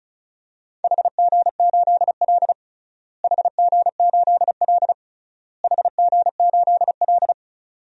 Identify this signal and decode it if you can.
Morse code, heard as a keyed tone, about 36 wpm, 700 Hz. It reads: HG8L HG8L HG8L
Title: DXCC-100 Call Signs - 4 Characters - Repeated 3x - 35wpm